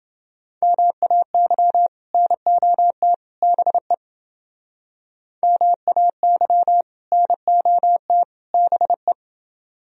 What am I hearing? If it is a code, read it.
Morse code, heard as a keyed tone, about 30 wpm, 700 Hz. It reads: MAY NOT BE MAY NOT BE